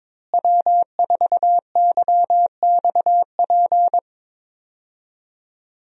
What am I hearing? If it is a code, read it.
Morse code, heard as a keyed tone, about 23 wpm, 700 Hz. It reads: W4YXP
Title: USA Call Signs - 5 Characters - Repeated 3x - 22wpm